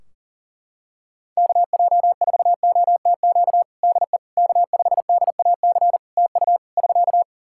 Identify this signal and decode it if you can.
Morse code, heard as a keyed tone, about 40 wpm, 700 Hz. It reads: KJ4OTQ DE K5DAC TU <SK>
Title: Practice QSO - Exchange 4 - Ending - 40wpm